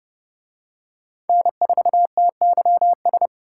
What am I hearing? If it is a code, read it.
Morse code, heard as a keyed tone, about 29 wpm, 700 Hz. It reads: N4TYS